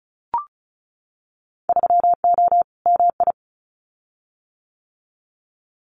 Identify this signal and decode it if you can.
Morse code, heard as a keyed tone, about 36 wpm, 700 Hz. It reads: 3O MI
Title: Winter Field Day Exchanges 35wpm